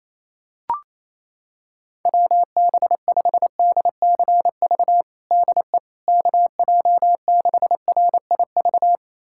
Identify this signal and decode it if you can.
Morse code, heard as a keyed tone, about 28 wpm, 700 Hz. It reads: WB5DCV DE KJ6RIV
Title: USA Callsign DE Callsign - 6 Characters - 28wpm